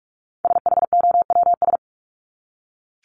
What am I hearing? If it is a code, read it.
Morse code, heard as a keyed tone, about 44 wpm, 700 Hz. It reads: SHOWS
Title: ES:S Suffix - Top 100 Words with Base Words 45wpm